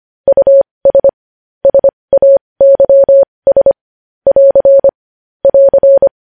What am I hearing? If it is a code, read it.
Morse code, heard as a keyed tone, about 25 wpm, spaced as for 20 wpm, 550 Hz. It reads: US SAYS <AR> <AR>
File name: podcast / News headlines-1x-25x20